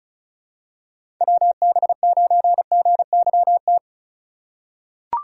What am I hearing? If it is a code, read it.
Morse code, heard as a keyed tone, about 36 wpm, 700 Hz. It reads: WB9GYT
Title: USA Call Signs - 6x Character Spacing - 35wpm